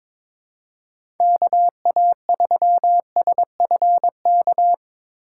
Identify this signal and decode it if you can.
Morse code, heard as a keyed tone, about 21 wpm, 700 Hz. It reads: KA3SFK
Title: USA Call Signs - 4x Character Spacing - 22wpm